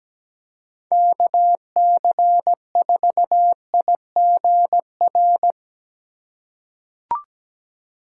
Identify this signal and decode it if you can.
Morse code, heard as a keyed tone, about 17 wpm, 700 Hz. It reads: KC4IGR
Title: USA Call Signs - 4x Character Spacing - 17wpm